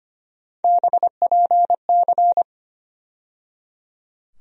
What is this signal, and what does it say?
Morse code, heard as a keyed tone, about 25 wpm, 700 Hz. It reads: BPC
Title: USA Call Sign Postfixes - 3 Letters 25wpm